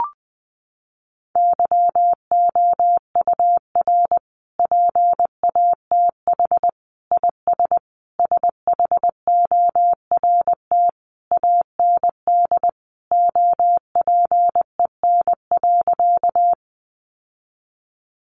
Morse code, heard as a keyed tone, about 20 wpm, 700 Hz. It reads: YOUR PATH IS SHORT AND OPEN.
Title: Sentences from Top 1400 Words 20wpm